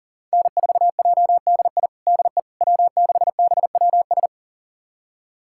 Morse code, heard as a keyed tone, about 40 wpm, 700 Hz. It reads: N4JDI DE W6BWS
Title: USA Callsign DE Callsign - 5 Characters - 40wpm